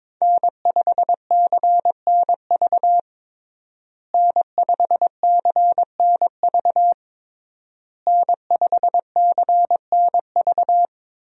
Morse code, heard as a keyed tone, about 22 wpm, 700 Hz. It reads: N5CNV N5CNV N5CNV
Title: USA Call Signs - 5 Characters - Repeated 3x - 22wpm